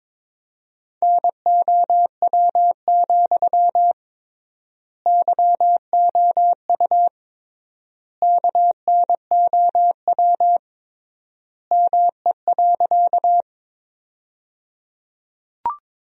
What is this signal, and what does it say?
Morse code, heard as a keyed tone, about 22 wpm, 700 Hz. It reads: NOW, YOU KNOW ME.
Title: 3x Word-Spacing - Sentences from Top 200 Words - 22wpm